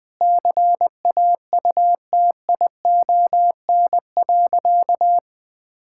Morse code, heard as a keyed tone, about 20 wpm, 700 Hz. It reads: CAUTION.
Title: Sentences with 7-Word Phrases - 20wpm